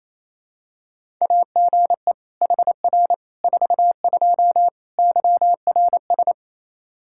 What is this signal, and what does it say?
Morse code, heard as a keyed tone, about 28 wpm, 700 Hz. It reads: AGE HR 42 YRS
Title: Practice QSO - Exchange 3 - More Info - 28wpm